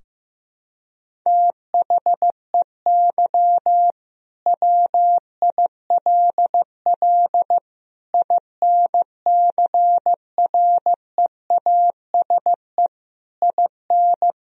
Morse code, heard as a keyed tone, about 15 wpm, 700 Hz. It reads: THEY WILL INCREASE IN
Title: Sentences from Top 1000 Words - Review Part 4 - 15wpm